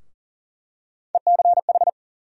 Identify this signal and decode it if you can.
Morse code, heard as a keyed tone, about 40 wpm, 700 Hz. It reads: ECH
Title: USA Call Sign Postfixes - 3 Letters 40wpm